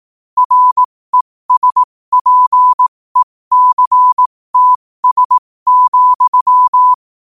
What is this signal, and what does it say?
Morse code, heard as a keyed tone, about 18 wpm, spaced as for 15 wpm, 1000 Hz. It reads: RESPECTS,
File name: qotdfiles / 15wpm